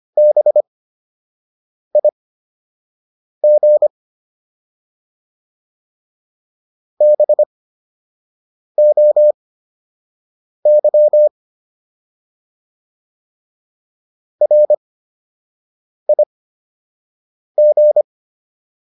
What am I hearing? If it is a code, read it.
Morse code, heard as a keyed tone, about 25 wpm, spaced as for 6 wpm, 600 Hz. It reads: BIG BOY RIG